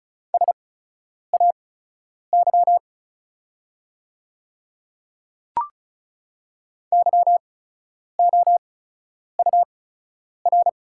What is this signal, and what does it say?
Morse code, heard as a keyed tone, about 36 wpm, spaced as for 10 wpm, 690 Hz. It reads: SAY YOUR
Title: 8x Character Spacing - Top 200 Words - 35wpm